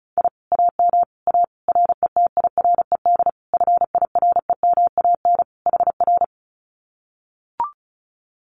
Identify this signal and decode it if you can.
Morse code, heard as a keyed tone, about 34 wpm, 700 Hz. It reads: I AM A RETIRED FIREMAN HR
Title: Common QSO Phrases Encore 35wpm